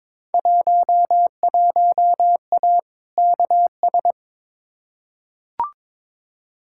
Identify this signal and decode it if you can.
Morse code, heard as a keyed tone, about 22 wpm, 700 Hz. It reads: 11A KS
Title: Field Day Exchanges Encore - 22wpm